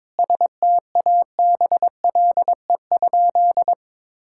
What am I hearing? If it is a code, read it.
Morse code, heard as a keyed tone, about 21 wpm, 700 Hz. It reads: STABLE?
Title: Sentences from Top 1600 Words 22wpm